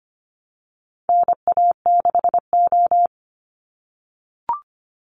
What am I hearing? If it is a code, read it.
Morse code, heard as a keyed tone, about 25 wpm, 700 Hz. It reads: NA6O
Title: NA SOTA Chaser Callsigns 25wpm